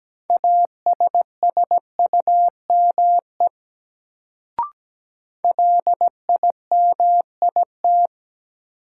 Morse code, heard as a keyed tone, about 17 wpm, 700 Hz. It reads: ASSUME LIMIT
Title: Sentences from Top 1500 Words - New Word Drill - 17wpm